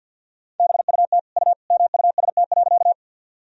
Morse code, heard as a keyed tone, about 48 wpm, 700 Hz. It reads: BUT U MUST.
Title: Sentences from Top 200 Words and Abbreviations 50wpm